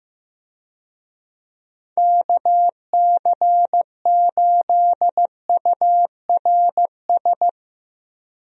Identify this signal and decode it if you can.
Morse code, heard as a keyed tone, about 15 wpm, 700 Hz. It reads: KC8URS
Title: USA Call Signs - 6 Characters - Repeated 3x - 15wpm